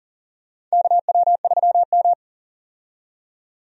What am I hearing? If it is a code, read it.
Morse code, heard as a keyed tone, about 37 wpm, 700 Hz. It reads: KW3M
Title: USA Call Signs - 4 Characters - 40wpm